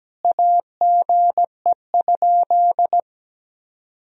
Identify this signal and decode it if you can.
Morse code, heard as a keyed tone, about 17 wpm, 700 Hz. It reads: AGE?
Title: Sentences from Top 1000 Words - Review Part 3 - 17wpm